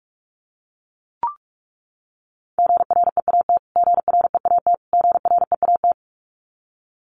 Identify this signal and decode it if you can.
Morse code, heard as a keyed tone, about 44 wpm, 700 Hz. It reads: GREAT GREAT GREAT
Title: Top 200 Words Repeated 45wpm